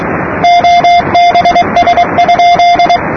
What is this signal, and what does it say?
Morse code, heard as a keyed tone, about 22 wpm, 700 Hz, with noise. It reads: OBS?
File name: namerica / usa / usa-8280-ksm-22445-8-khz-15-03-2008-2125-utc-gschwartz